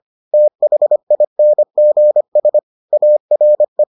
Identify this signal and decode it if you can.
Morse code, heard as a keyed tone, about 25 wpm, 600 Hz. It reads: THINGS ARE